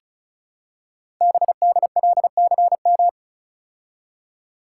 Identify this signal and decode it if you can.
Morse code, heard as a keyed tone, about 36 wpm, 700 Hz. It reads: BDLCM
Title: Code Groups - 5 Letters - 35wpm